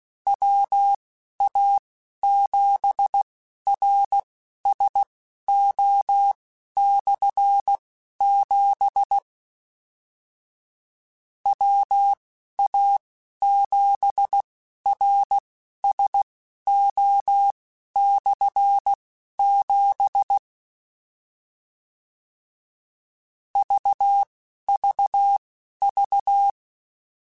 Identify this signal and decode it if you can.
Morse code, heard as a keyed tone, about 16 wpm, spaced as for 11 wpm, 800 Hz. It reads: WA7RSO/7 WA7RSO/7 VVV
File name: Combinations / Combinations-mp3s / Combinations@10